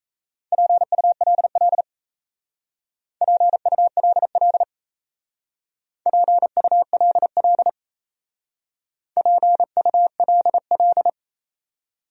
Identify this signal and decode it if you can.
Morse code, heard as a keyed tone, about 35 wpm, 700 Hz. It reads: PULL PULL PULL PULL